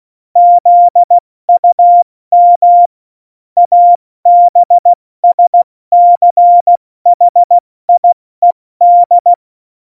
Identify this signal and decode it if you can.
Morse code, heard as a keyed tone, about 16 wpm, 700 Hz. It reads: ZUM ABSCHIED
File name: Morsekurs / Witze / Witz 14 Juli 2020, 70 F80